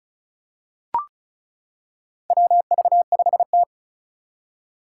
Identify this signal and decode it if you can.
Morse code, heard as a keyed tone, about 36 wpm, 700 Hz. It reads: WV5T
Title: Four Challenging Letter-Number Combinations 35wpm